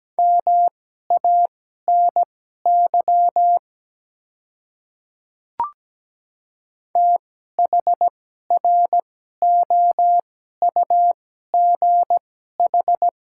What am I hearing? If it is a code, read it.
Morse code, heard as a keyed tone, about 17 wpm, spaced as for 12 wpm, 700 Hz. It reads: MANY THROUGH
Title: 2x Character Spacing - Top 300 Words - 17wpm